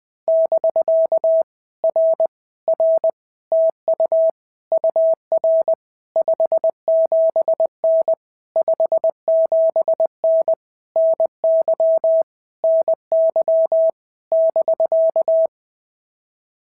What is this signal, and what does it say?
Morse code, heard as a keyed tone, about 20 wpm, 650 Hz. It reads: <BK> R R TU UR 57N 57N NY NY <BK>
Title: POTA QSO Practice - 20wpm